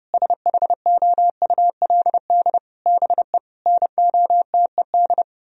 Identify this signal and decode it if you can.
Morse code, heard as a keyed tone, about 36 wpm, 700 Hz. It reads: SHOULD BE NOTED
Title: Speed-Racing - Sets of 5 Words 30wpm